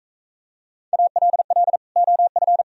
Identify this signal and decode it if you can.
Morse code, heard as a keyed tone, about 40 wpm, 700 Hz. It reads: ALL OF